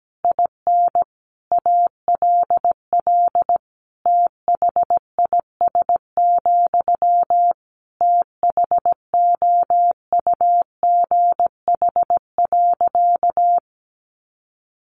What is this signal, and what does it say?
Morse code, heard as a keyed tone, about 17 wpm, 700 Hz. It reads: IN ALL THIS, THOUGH.
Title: Sentences from Top 1400 Words 17wpm